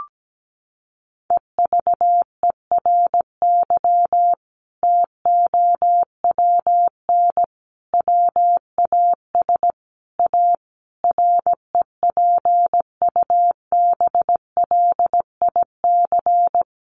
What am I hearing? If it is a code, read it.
Morse code, heard as a keyed tone, about 17 wpm, 700 Hz. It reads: EVERY TOWN WAS A REPUBLIC
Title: Sentences from Top 1500 Words 17wpm